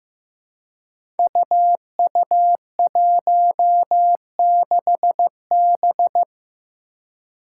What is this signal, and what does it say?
Morse code, heard as a keyed tone, about 15 wpm, 700 Hz. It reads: UU16B
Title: Five Challenging Letter-Number Combinations 15wpm